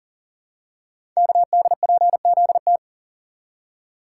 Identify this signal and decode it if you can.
Morse code, heard as a keyed tone, about 38 wpm, 700 Hz. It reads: KDPZT
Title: Code Groups - 5 Letters - 40wpm